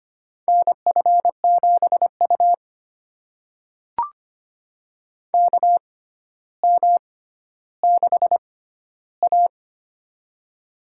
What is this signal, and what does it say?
Morse code, heard as a keyed tone, about 25 wpm, spaced as for 13 wpm, 700 Hz. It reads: NF7U KM6A
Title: USA Call Signs - 4 Characters - 6x Character Spacing - 25wpm